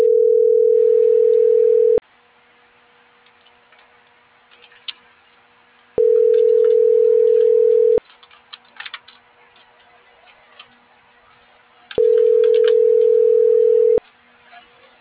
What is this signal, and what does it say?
Ringback tone (North America), 440 + 480 Hz.
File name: CENTRAL MABE / out miriam.soledad